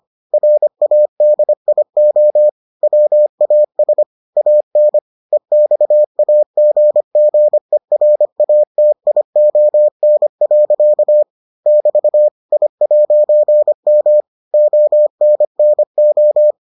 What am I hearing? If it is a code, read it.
Morse code, heard as a keyed tone, about 25 wpm, 600 Hz. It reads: RADIO WAS AN EXAGGERATION. <BT> I'M ONNO